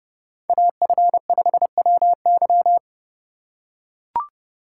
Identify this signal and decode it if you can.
Morse code, heard as a keyed tone, about 29 wpm, 700 Hz. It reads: AF5WY